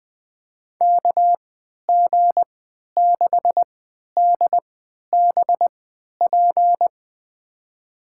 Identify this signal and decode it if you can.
Morse code, heard as a keyed tone, about 20 wpm, spaced as for 11 wpm, 700 Hz. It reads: KG6DBP
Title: USA Call Signs - 6 Characters - 3x Character Spacing - 20wpm